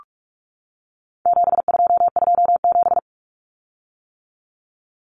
Morse code, heard as a keyed tone, about 44 wpm, 700 Hz. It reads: 7227
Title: Common Cut Numbers - 4-Digit 45wpm